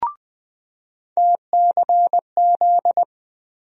Morse code, heard as a keyed tone, about 20 wpm, 700 Hz. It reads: TCZ
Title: USA Call Sign Postfixes - 3 Letters 20wpm